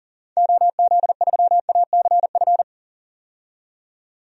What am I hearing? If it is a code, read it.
Morse code, heard as a keyed tone, about 39 wpm, 700 Hz. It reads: OZ3ACF
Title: DXCC-100 Call Signs - 6 Characters - 40wpm